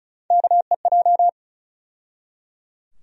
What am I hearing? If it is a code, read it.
Morse code, heard as a keyed tone, about 34 wpm, 700 Hz. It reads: KEJ
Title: USA Call Sign Postfixes - 3 Letters 35wpm